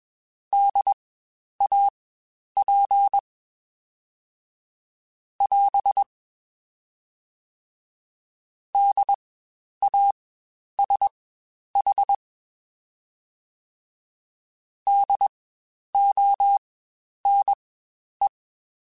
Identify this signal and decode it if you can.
Morse code, heard as a keyed tone, about 21 wpm, spaced as for 10 wpm, 800 Hz. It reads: DAP <AS> DASH DONE
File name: Spaced Alphas / Spaced Alphas mp3s / Ds sp @20